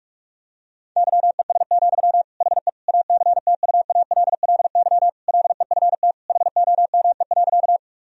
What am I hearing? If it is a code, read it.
Morse code, heard as a keyed tone, about 46 wpm, 700 Hz. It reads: YES, HE ACTUALLY LEFT HOME.
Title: Sentences from Top 1000 Words - Review Part 4 - 45wpm